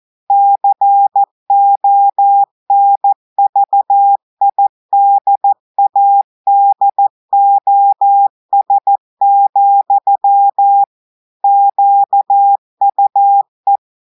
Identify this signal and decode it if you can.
Morse code, heard as a keyed tone, about 14 wpm, 800 Hz. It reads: CONVIDADOS, QUE
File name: audios / portugais / portugais texte en clair 1 vitesse 14 mots minutes